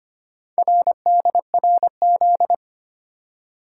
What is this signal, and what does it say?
Morse code, heard as a keyed tone, about 25 wpm, 700 Hz. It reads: RDRZ
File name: practice / 25wpm / Four Letter-Number Combinations - 25wpm